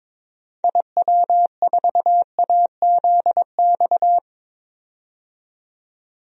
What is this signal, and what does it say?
Morse code, heard as a keyed tone, about 22 wpm, 700 Hz. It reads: IW4AZX
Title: DXCC-100 Call Signs - 6 Characters - 22wpm